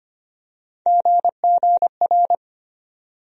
Morse code, heard as a keyed tone, about 25 wpm, 700 Hz. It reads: GGR
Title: USA Call Sign Postfixes - 3 Letters 25wpm